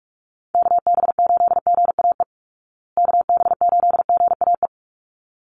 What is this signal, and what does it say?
Morse code, heard as a keyed tone, about 44 wpm, 700 Hz. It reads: KB8GAE KB8GAE
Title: Call Signs Repeated 3x - 45wpm